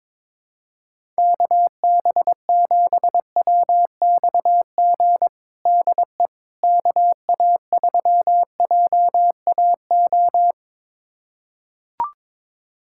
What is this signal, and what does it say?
Morse code, heard as a keyed tone, about 22 wpm, 700 Hz. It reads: KB7WXG DE KA3JAO
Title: USA Callsign DE Callsign 22wpm